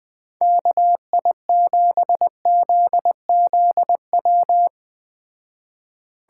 Morse code, heard as a keyed tone, about 20 wpm, 700 Hz. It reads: KI7ZZW